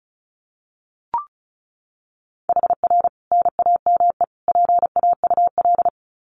Morse code, heard as a keyed tone, about 34 wpm, 700 Hz. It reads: HR NAME PAUL
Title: Common QSO Phrases 35wpm